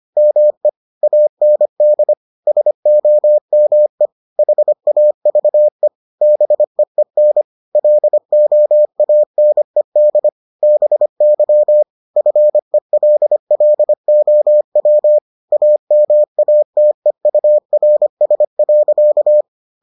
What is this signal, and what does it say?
Morse code, heard as a keyed tone, about 25 wpm, 600 Hz. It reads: ME AND SOME HAVE BEEN LOANED BY FELLOW AMATEURS.